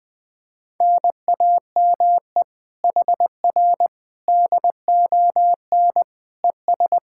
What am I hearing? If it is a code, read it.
Morse code, heard as a keyed tone, about 20 wpm, 700 Hz. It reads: NAME HR DON ES
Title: Speed-Racing Common QSO Phrases - 17wpm